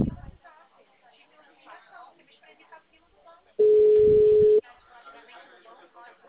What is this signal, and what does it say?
Ringback tone (Europe, 425 Hz), with speech.